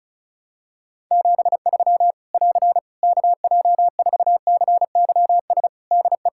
Morse code, heard as a keyed tone, about 36 wpm, 700 Hz. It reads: 73 <AR> KJ4CYS DE
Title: Practice QSO - Exchange 4 - Ending - 35wpm